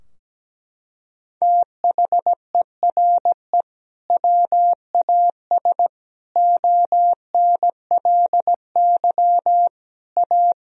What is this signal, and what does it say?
Morse code, heard as a keyed tone, about 17 wpm, 700 Hz. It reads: THERE WAS ONLY A
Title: Sentences from Top 1000 Words - Review Part 2 - 17wpm